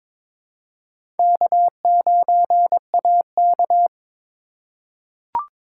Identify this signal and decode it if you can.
Morse code, heard as a keyed tone, about 22 wpm, 700 Hz. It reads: K9AK
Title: USA Call Signs - 4 Characters - 2x Character Spacing - 22wpm